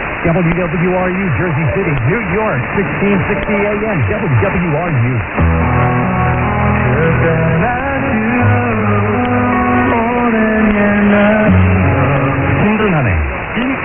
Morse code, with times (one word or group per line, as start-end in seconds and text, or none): none